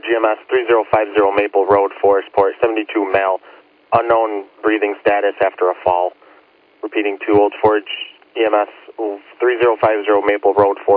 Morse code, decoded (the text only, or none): none